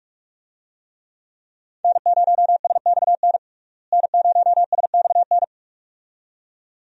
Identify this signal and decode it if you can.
Morse code, heard as a keyed tone, about 46 wpm, 700 Hz. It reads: N0SXN N0SXN
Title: USA Call Signs - 5 Characters - Repeated 2x - 45wpm